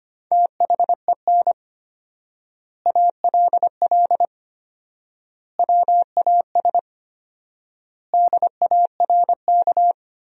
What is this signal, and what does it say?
Morse code, heard as a keyed tone, about 25 wpm, 700 Hz. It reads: THEN ALL WAS DARK
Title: 4x Word-Spacing - Sentences from Top 500 Words - 25wpm